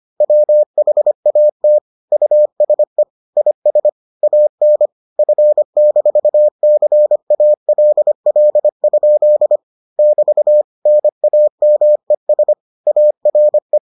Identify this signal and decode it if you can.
Morse code, heard as a keyed tone, about 25 wpm, 600 Hz. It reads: WHAT USE IS AN F-CALL? <BT> NAMES ARE